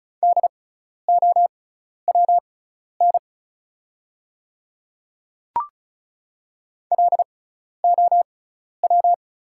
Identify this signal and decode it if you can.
Morse code, heard as a keyed tone, about 36 wpm, spaced as for 12 wpm, 690 Hz. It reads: DOWN LOW
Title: 6x Character Spacing - Top 300 Words - 35wpm